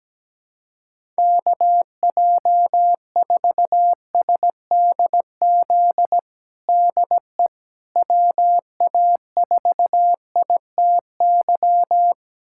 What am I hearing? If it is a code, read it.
Morse code, heard as a keyed tone, about 17 wpm, 700 Hz. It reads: KJ4SDZ DE WA4ITY
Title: USA Callsign DE Callsign - 6 Characters - 17wpm